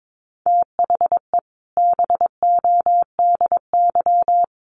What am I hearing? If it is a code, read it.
Morse code, heard as a keyed tone, about 22 wpm, 700 Hz. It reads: THE BODY
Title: A Sound Of Thunder 22wpm